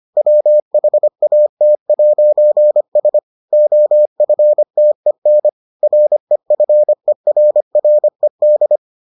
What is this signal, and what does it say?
Morse code, heard as a keyed tone, about 25 wpm, 600 Hz. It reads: WHAT'S OFTEN REFERRED